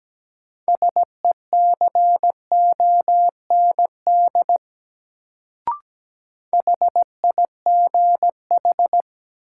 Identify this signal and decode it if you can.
Morse code, heard as a keyed tone, about 17 wpm, 700 Hz. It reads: SECOND HIGH
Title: Sets of 2 Words - New Word Drill - 17wpm